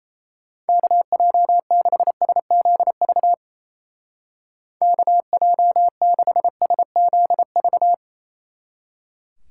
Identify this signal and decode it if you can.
Morse code, heard as a keyed tone, about 37 wpm, 700 Hz. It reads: KJ6SZV KJ6SZV